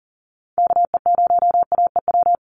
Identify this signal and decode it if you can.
Morse code, heard as a keyed tone, about 40 wpm, 700 Hz. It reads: KE0AEW
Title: USA Call Signs 40wpm